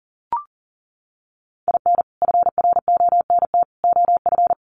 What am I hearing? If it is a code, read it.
Morse code, heard as a keyed tone, about 40 wpm, 700 Hz. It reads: IN FRONT OF
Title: Sets of 3 Words 40wpm